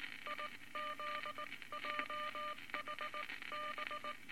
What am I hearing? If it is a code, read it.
Morse code, heard as a keyed tone, about 20 wpm, spaced as for 17 wpm, 1200 Hz, with noise. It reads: IZJHB
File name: Morse Code Records / 19 B LCM 167 Twelve Words Per Minute